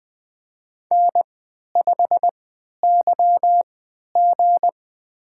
Morse code, heard as a keyed tone, about 20 wpm, spaced as for 11 wpm, 700 Hz. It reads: N5YG
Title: USA Call Signs - 4 Characters - 3x Character Spacing - 20wpm